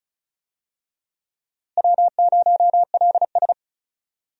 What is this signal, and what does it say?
Morse code, heard as a keyed tone, about 34 wpm, 700 Hz. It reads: W0LS
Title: USA Call Signs - 4 Characters - Repeated 2x - 35wpm